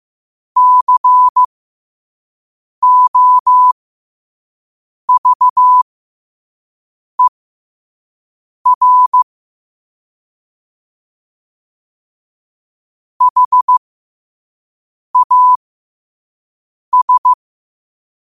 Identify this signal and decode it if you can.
Morse code, heard as a keyed tone, about 15 wpm, spaced as for 5 wpm, 1000 Hz. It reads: COVER HAS
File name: qotdfiles / 5wpm